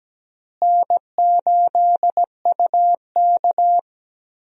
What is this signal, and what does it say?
Morse code, heard as a keyed tone, about 17 wpm, 700 Hz. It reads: N8UK
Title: USA Call Signs - 4 Characters - Repeated 3x - 17wpm